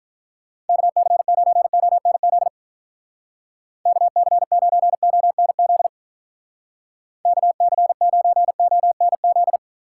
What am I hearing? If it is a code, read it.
Morse code, heard as a keyed tone, about 45 wpm, 700 Hz. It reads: KC9ONZ KC9ONZ KC9ONZ